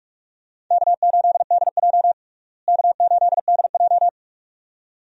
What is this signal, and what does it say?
Morse code, heard as a keyed tone, about 41 wpm, 700 Hz. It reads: K8DJ K8DJ
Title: USA Call Signs - 4 Characters - Repeated 2x - 45wpm